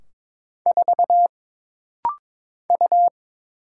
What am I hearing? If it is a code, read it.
Morse code, heard as a keyed tone, about 22 wpm, 700 Hz. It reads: 4 U
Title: Single Letter-Number Alt - Rapid-Fire 22wpm